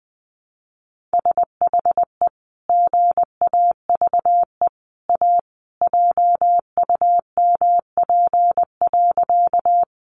Morse code, heard as a keyed tone, about 19 wpm, 700 Hz. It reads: SHE GAVE A JUMP.
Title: Sentences from Top 1300 Words 20wpm